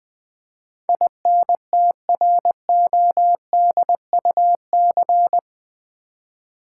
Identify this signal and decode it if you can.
Morse code, heard as a keyed tone, about 20 wpm, 700 Hz. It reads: INTRODUC
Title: Build A Word - 4 Letters At A Time - 12-Letter Words - 20wpm